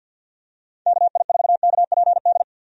Morse code, heard as a keyed tone, about 50 wpm, 700 Hz. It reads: KI4KPD
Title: USA Call Signs - 6 Characters - 50wpm